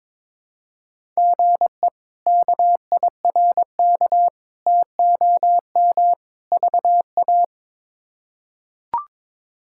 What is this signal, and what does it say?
Morse code, heard as a keyed tone, about 22 wpm, 700 Hz. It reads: GE KIRK TOM VA
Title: K1USN SST QSO Practice - 22wpm